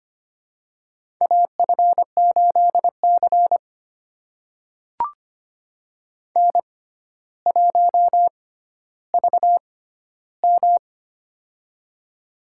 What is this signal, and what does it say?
Morse code, heard as a keyed tone, about 25 wpm, spaced as for 13 wpm, 700 Hz. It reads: AF8C N1VM
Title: USA Call Signs - 4 Characters - 6x Character Spacing - 25wpm